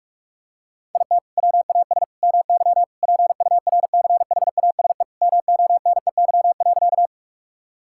Morse code, heard as a keyed tone, about 41 wpm, 700 Hz. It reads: IT WAS MY PURCHASE MONEY.
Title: Sentences from Top 1000 Words - Review Part 5 - 45wpm